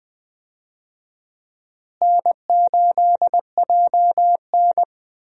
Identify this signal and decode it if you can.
Morse code, heard as a keyed tone, about 20 wpm, 700 Hz. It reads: N8JN
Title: USA Call Signs - 4 Characters - Repeated 2x - 20wpm